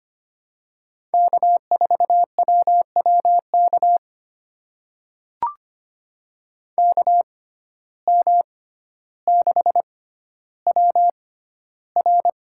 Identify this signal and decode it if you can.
Morse code, heard as a keyed tone, about 25 wpm, spaced as for 13 wpm, 700 Hz. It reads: K4WWK KM6WR
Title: USA Call Signs - 5 Characters - 6x Character Spacing - 25wpm